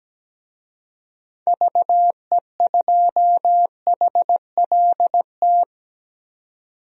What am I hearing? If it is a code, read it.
Morse code, heard as a keyed tone, about 17 wpm, 700 Hz. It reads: VE2HLT
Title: DXCC-100 Call Signs - 6 Characters - 17wpm